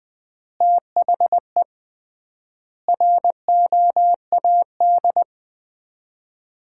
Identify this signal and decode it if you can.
Morse code, heard as a keyed tone, about 20 wpm, 700 Hz. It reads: THE ROAD
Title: 3x Word-Spacing - Sets of 5 Words - 20wpm